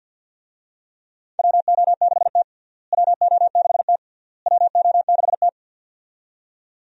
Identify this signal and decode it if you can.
Morse code, heard as a keyed tone, about 46 wpm, 700 Hz. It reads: WO6T WO6T WO6T
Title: USA Call Signs - 4 Characters - Repeated 3x - 50wpm